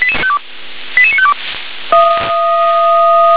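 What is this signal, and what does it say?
Five-tone selective call (ZVEI), heard as quick runs of five tones, tones of about 70 ms. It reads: 88952
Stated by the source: ZVEI five-tone selective call received on SDR.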